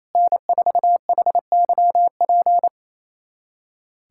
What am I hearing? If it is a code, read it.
Morse code, heard as a keyed tone, about 28 wpm, 700 Hz. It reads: N4HYP
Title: USA Callsign DE Callsign - 5 Characters - 28wpm